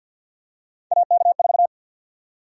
Morse code, heard as a keyed tone, about 53 wpm, 700 Hz. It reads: AK4
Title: USA Call Sign Prefixes - 50wpm